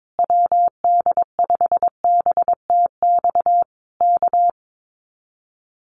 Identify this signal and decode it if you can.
Morse code, heard as a keyed tone, about 22 wpm, 700 Hz. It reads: WD5BTX K
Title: Common QSO Phrases Encore 22wpm